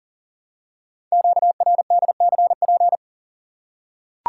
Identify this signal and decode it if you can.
Morse code, heard as a keyed tone, about 42 wpm, 700 Hz. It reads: QRDCP
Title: Code Groups - 5 Letter:Numbers - 40wpm